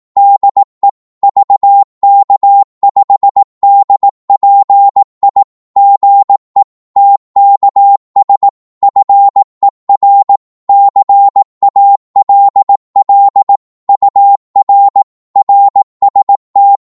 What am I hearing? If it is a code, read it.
Morse code, heard as a keyed tone, about 18 wpm, 800 Hz. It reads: DE VK5DPI GE TKS FER CALL UR RST
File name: Morse / qso-18wpm